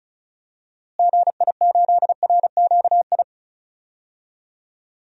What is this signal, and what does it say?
Morse code, heard as a keyed tone, about 34 wpm, 700 Hz. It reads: GI8RQI
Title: DXCC-100 Call Signs - 6 Characters - Repeated 2x - 35wpm